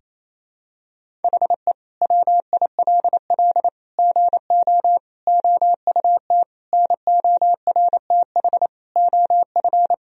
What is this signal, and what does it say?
Morse code, heard as a keyed tone, about 28 wpm, 700 Hz. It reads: HE WILL GO OUT NORTH OF
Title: Sentences from Top 1000 Words - Review Part 1 - 28wpm